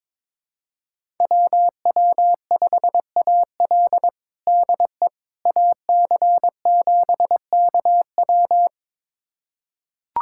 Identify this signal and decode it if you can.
Morse code, heard as a keyed tone, about 22 wpm, 700 Hz. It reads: WW5AL DE AC7KW
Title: USA Callsign DE Callsign - 5 Characters - 22wpm